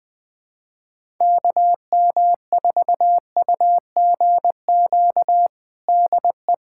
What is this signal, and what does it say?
Morse code, heard as a keyed tone, about 21 wpm, 700 Hz. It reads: KM4UGQ DE
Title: USA Callsign DE Callsign 20wpm